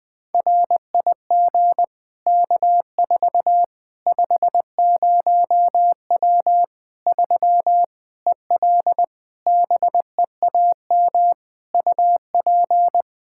Morse code, heard as a keyed tone, about 20 wpm, 700 Hz. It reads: RIG K4 50W 3 EL BEAM UP
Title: Practice QSO - Exchange 2 - Info - 20wpm